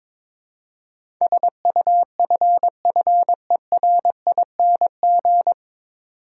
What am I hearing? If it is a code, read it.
Morse code, heard as a keyed tone, about 22 wpm, 700 Hz. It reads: SUFFERING
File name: 9 Letter Words - First 3 Letters Given - 22wpm